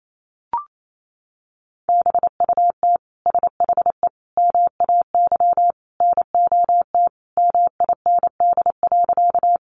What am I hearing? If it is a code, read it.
Morse code, heard as a keyed tone, about 28 wpm, 700 Hz. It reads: BUT SHE MAY NOT MIND.
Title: Sentences from Top 600 Words 28wpm